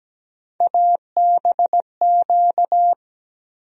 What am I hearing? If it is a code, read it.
Morse code, heard as a keyed tone, about 17 wpm, 700 Hz. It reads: ABQ
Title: USA Call Sign Postfixes - 3 Letters 17wpm